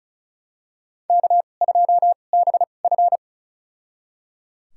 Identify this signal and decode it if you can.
Morse code, heard as a keyed tone, about 36 wpm, spaced as for 25 wpm, 700 Hz. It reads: K2BF
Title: USA Call Signs - 4 Characters - 2x Character Spacing - 35wpm